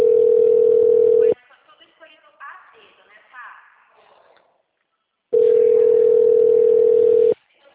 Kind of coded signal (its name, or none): telephone call progress tone